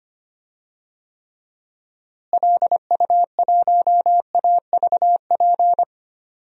Morse code, heard as a keyed tone, about 25 wpm, 700 Hz. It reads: LU1AVP
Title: DXCC-100 Call Signs - 6 Characters - Repeated 3x - 25wpm